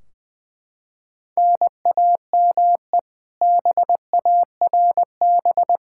Morse code, heard as a keyed tone, about 20 wpm, 700 Hz. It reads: NAME BARB
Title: K1USN SST QSO Practice 20wpm